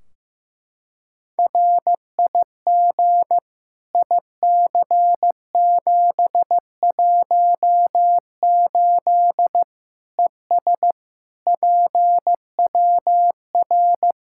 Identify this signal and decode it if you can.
Morse code, heard as a keyed tone, about 16 wpm, 700 Hz. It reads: RIG IC718 ES PWR
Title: Practice QSO - Exchange 2 - Info - 15wpm